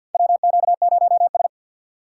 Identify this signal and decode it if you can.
Morse code, heard as a keyed tone, about 51 wpm, 700 Hz. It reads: WQ0S
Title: USA Call Signs - 3x Character Spacing - 50wpm